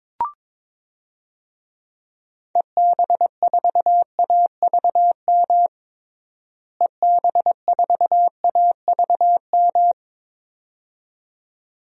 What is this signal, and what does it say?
Morse code, heard as a keyed tone, about 22 wpm, 700 Hz. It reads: EB4AVM EB4AVM
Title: DXCC-100 Call Signs - 6 Characters - Repeated 2x - 22wpm